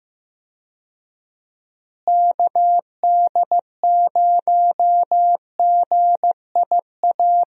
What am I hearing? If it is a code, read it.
Morse code, heard as a keyed tone, about 15 wpm, 700 Hz. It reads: KD0GIA
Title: USA Call Signs - 6 Characters - Repeated 3x - 15wpm